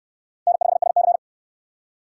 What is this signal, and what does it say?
Morse code, heard as a keyed tone, about 69 wpm, 700 Hz. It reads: N5SC